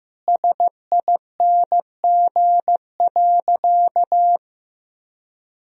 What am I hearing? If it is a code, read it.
Morse code, heard as a keyed tone, about 15 wpm, 700 Hz. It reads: SING.
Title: Sentences from Top 1000 Words - Review Part 2 - 15wpm